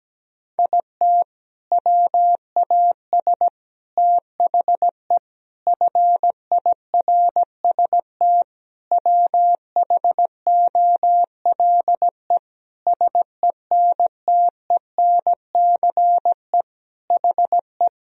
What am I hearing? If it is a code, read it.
Morse code, heard as a keyed tone, about 17 wpm, 700 Hz. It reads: IT WAS THE FIRST WHOLE SENTENCE HE
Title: Sentences from Top 1000 Words - Review Part 1 - 17wpm